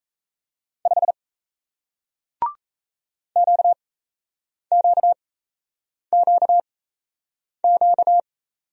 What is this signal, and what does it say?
Morse code, heard as a keyed tone, about 38 wpm, spaced as for 8 wpm, 700 Hz. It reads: 5 QQQQ